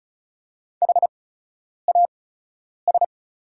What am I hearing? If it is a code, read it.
Morse code, heard as a keyed tone, about 36 wpm, spaced as for 10 wpm, 690 Hz. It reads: HAS
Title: 8x Character Spacing - Top 200 Words - 35wpm